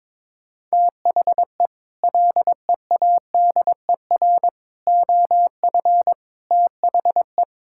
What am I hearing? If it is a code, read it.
Morse code, heard as a keyed tone, about 22 wpm, 700 Hz. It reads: THE LEADER OF THE
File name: practice / 22wpm / Sets of 6 Words - 22wpm